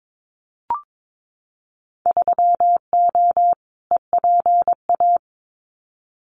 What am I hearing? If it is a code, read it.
Morse code, heard as a keyed tone, about 22 wpm, 700 Hz. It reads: 3O EPA
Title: Winter Field Day Exchanges 22wpm